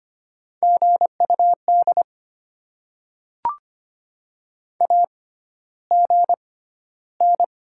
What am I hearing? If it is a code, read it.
Morse code, heard as a keyed tone, about 25 wpm, spaced as for 13 wpm, 690 Hz. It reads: GUD AGN
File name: Top 20 QSO Elements - 6x Character Spacing - 25wpm